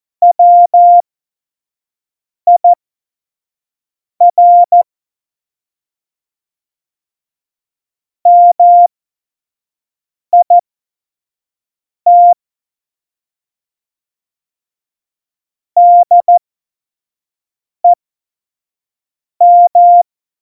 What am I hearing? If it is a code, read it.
Morse code, heard as a keyed tone, about 14 wpm, spaced as for 5 wpm, 700 Hz. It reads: WIR MIT DEM
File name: Morsekurs / Witze / Witz 07. November 2017, 25 F70